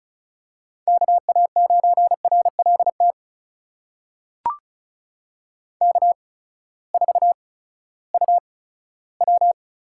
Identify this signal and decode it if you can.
Morse code, heard as a keyed tone, about 36 wpm, spaced as for 18 wpm, 700 Hz. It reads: KA9RLT K4UW
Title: USA Call Signs - 8x Character Spacing - 35wpm